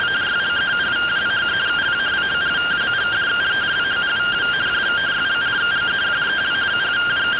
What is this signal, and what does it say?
Two-tone teleprinter signal, about 170 Hz shift.